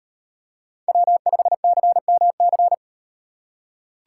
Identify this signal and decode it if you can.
Morse code, heard as a keyed tone, about 36 wpm, 700 Hz. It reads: W5CMC